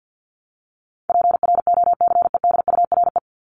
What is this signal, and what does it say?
Morse code, heard as a keyed tone, about 50 wpm, 700 Hz. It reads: PROCEDURE
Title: Top 1751-2000 Words 50wpm